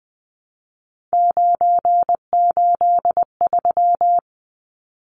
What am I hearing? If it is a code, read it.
Morse code, heard as a keyed tone, about 20 wpm, 700 Hz. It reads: 983
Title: Numbers - 3-Digit 20wpm